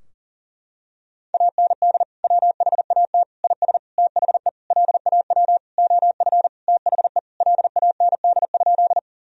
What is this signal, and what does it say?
Morse code, heard as a keyed tone, about 37 wpm, 700 Hz. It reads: AND WHAT IS THE LAW OF THE LAND?
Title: Sentences from Top 1000 Words - Review Part 3 - 40wpm